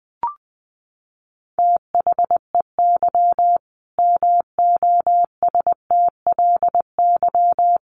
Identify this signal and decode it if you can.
Morse code, heard as a keyed tone, about 20 wpm, 700 Hz. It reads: THEY MOSTLY
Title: Sentences from Top 1300 Words 20wpm